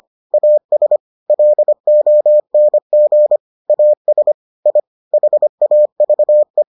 Morse code, heard as a keyed tone, about 25 wpm, 600 Hz. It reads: AS LONG AS I HAVE